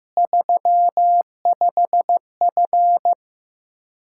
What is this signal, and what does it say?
Morse code, heard as a keyed tone, about 16 wpm, 700 Hz. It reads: 35F
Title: Practice QSO - Exchange 2 - Info - 15wpm